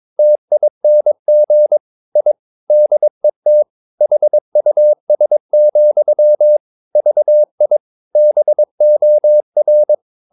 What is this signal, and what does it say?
Morse code, heard as a keyed tone, about 22 wpm, 600 Hz. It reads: TING I DET HUS, VI BOR
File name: Ferie QRS 22wpm